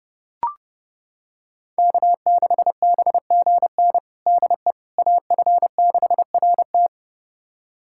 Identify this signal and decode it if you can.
Morse code, heard as a keyed tone, about 32 wpm, 700 Hz. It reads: K6BGN DE AF6RT
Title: USA Callsign DE Callsign - 5 Characters - 30wpm